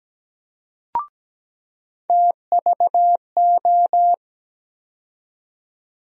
Morse code, heard as a keyed tone, about 17 wpm, 700 Hz. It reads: TVO
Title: USA Call Sign Postfixes - 3 Letters 17wpm